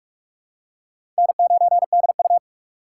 Morse code, heard as a keyed tone, about 45 wpm, 700 Hz. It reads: N9DU
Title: USA Call Signs - 4 Characters - 2x Character Spacing - 45wpm